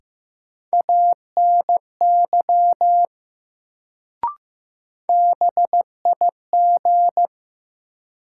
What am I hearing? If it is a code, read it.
Morse code, heard as a keyed tone, about 15 wpm, 700 Hz. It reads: ANY BIG
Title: Sentences from Top 200 Words - New Word Drill - 15wpm